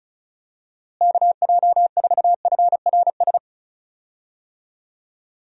Morse code, heard as a keyed tone, about 34 wpm, 700 Hz. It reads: KJ4FRS
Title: USA Call Signs - 6 Characters - Repeated 3x - 35wpm